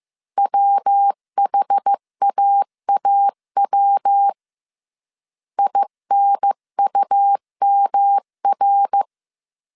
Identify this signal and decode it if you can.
Morse code, heard as a keyed tone, about 15 wpm, 800 Hz. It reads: WHAAW INUMR
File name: CW / DET / 1 3 1500C S A 14wpm 2447